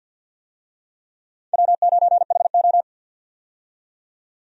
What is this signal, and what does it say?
Morse code, heard as a keyed tone, about 53 wpm, 700 Hz. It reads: W9HO
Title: USA Call Signs - 4 Characters - 50wpm